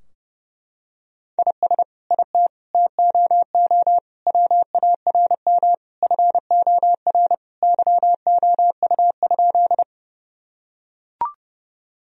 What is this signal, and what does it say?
Morse code, heard as a keyed tone, about 30 wpm, 700 Hz. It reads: IS IT TOO WARM FOR YOU?
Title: Sentences from Top 1000 Words - Review Part 2 - 30wpm